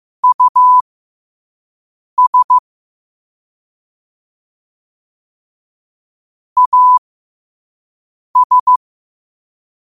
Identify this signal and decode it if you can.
Morse code, heard as a keyed tone, about 15 wpm, spaced as for 5 wpm, 1000 Hz. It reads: US AS